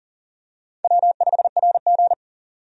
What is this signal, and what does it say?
Morse code, heard as a keyed tone, about 38 wpm, 700 Hz. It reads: W5RG
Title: USA Call Signs - 4 Characters - 40wpm